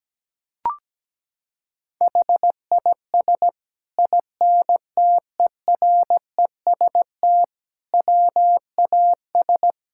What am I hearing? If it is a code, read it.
Morse code, heard as a keyed tone, about 17 wpm, 700 Hz. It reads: HIS INTEREST WAS
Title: Sentences from Top 1000 Words - Review Part 5 - 17wpm